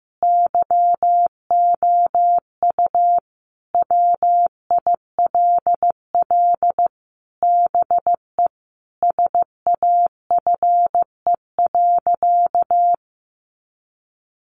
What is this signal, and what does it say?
Morse code, heard as a keyed tone, about 15 wpm, 700 Hz. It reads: YOU WILL BE SAFE.
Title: Sentences from Top 1300 Words 15wpm